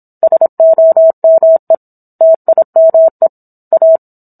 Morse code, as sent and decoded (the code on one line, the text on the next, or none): ... --- -- . / - .. -- . / .-
SOME TIME A